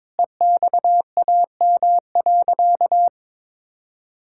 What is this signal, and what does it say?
Morse code, heard as a keyed tone, about 21 wpm, 700 Hz. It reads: EXAM.
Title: Sentences from Top 1000 Words - Review Part 5 - 22wpm